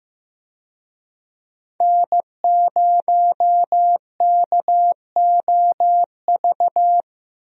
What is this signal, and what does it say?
Morse code, heard as a keyed tone, about 15 wpm, 700 Hz. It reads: N0KOV
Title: USA Call Signs - 5 Characters - Repeated 3x - 15wpm